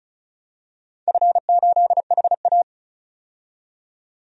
Morse code, heard as a keyed tone, about 34 wpm, 700 Hz. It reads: F8HA
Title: DXCC-100 Call Signs - 4 Characters - 35wpm